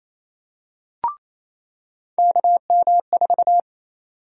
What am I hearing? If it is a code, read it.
Morse code, heard as a keyed tone, about 28 wpm, 700 Hz. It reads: KM4
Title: USA Call Sign Prefixes - 28wpm